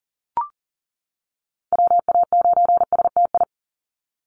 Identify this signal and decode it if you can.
Morse code, heard as a keyed tone, about 40 wpm, 700 Hz. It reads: WA9STI
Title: NA SOTA Activator Callsigns 40wpm